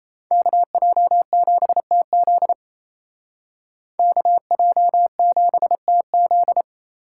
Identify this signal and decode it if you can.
Morse code, heard as a keyed tone, about 35 wpm, 700 Hz. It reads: KJ7TZ KJ7TZ